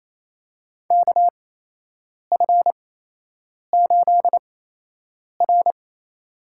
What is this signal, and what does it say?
Morse code, heard as a keyed tone, about 28 wpm, spaced as for 8 wpm, 700 Hz. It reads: KF8R
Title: USA Call Signs - 4 Characters - 8x Character Spacing - 28wpm